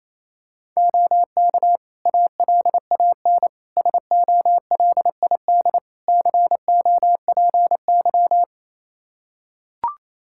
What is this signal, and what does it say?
Morse code, heard as a keyed tone, about 28 wpm, 700 Hz. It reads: OK ALAN SOLID COPY
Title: Practice QSO - Exchange 3 - More Info - 28wpm